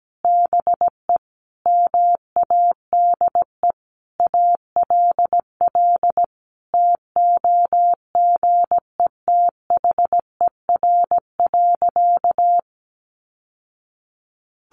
Morse code, heard as a keyed tone, about 17 wpm, 700 Hz. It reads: BE MADE ALL TOGETHER.
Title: Sentences from Top 1100 Words 17wpm